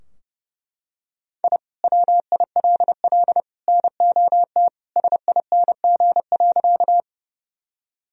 Morse code, heard as a keyed tone, about 32 wpm, 700 Hz. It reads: I WILL NOT SING.
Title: Sentences from Top 1000 Words - Review Part 2 - 30wpm